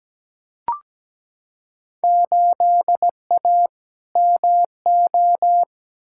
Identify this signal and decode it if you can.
Morse code, heard as a keyed tone, about 17 wpm, 700 Hz. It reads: 8A MO
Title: Field Day Exchanges Encore - 17wpm